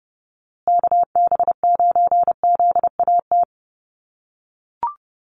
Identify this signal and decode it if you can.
Morse code, heard as a keyed tone, about 30 wpm, 700 Hz. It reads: KB9ZAT
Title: USA Call Signs Repeated 3x 30wpm